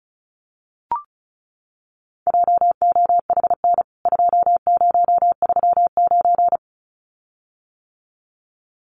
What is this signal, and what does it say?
Morse code, heard as a keyed tone, about 34 wpm, 700 Hz. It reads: JOHN 2039
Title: CWT Exchange 35wpm